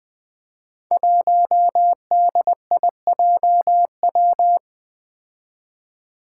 Morse code, heard as a keyed tone, about 20 wpm, 700 Hz. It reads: 1DIJW
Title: Five Challenging Letter-Number Combinations 20wpm